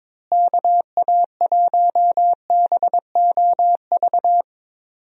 Morse code, heard as a keyed tone, about 22 wpm, 700 Hz. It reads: KA1BOV
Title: USA Call Signs - 6 Characters - Repeated 3x - 22wpm